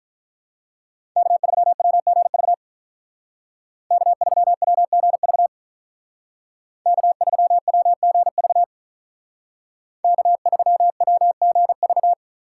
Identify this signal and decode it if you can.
Morse code, heard as a keyed tone, about 44 wpm, 700 Hz. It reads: K3WGV K3WGV K3WGV K3WGV